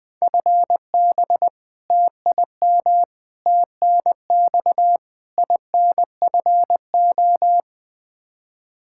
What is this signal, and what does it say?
Morse code, heard as a keyed tone, about 21 wpm, 700 Hz. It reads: FB TIM TNX INFO
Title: Practice QSO - Exchange 3 - More Info - 20wpm